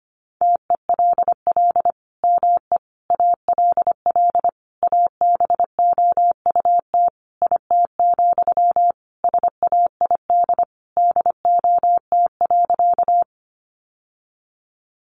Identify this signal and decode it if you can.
Morse code, heard as a keyed tone, about 25 wpm, 700 Hz. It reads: TELL ME ALL ABOUT IT, SAID DOT.
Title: Sentences from Top 1500 Words 25wpm